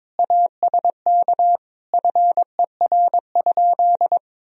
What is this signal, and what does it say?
Morse code, heard as a keyed tone, about 22 wpm, 700 Hz. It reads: ASK FER?
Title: Sentences from Top 200 Words and Abbreviations 22wpm